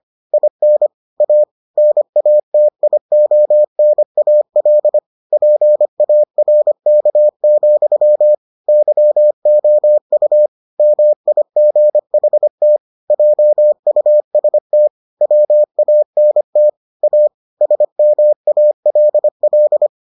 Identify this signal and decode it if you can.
Morse code, heard as a keyed tone, about 25 wpm, 600 Hz. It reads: IN A NATIONAL PARK, YOU MIGHT JUST WANT A SMALL